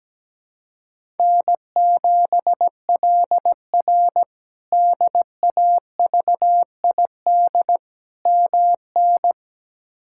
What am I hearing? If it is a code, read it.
Morse code, heard as a keyed tone, about 17 wpm, 700 Hz. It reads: N7LR DAVID MN
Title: K1USN SST QSO Practice - 17wpm